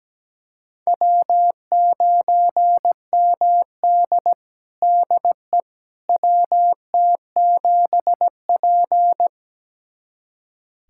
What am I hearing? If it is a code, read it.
Morse code, heard as a keyed tone, about 17 wpm, 700 Hz. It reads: W9MD DE WT7P
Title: USA Callsign DE Callsign - 4 Characters - 17wpm